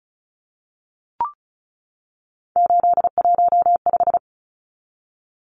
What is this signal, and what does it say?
Morse code, heard as a keyed tone, about 34 wpm, 700 Hz. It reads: 815
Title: Numbers - 3-Digit 35wpm